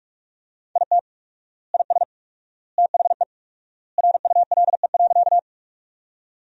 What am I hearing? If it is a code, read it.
Morse code, heard as a keyed tone, about 46 wpm, 700 Hz. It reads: IT IS THE RULE.
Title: 4x Word-Spacing - Sentences from Top 400 Words - 45wpm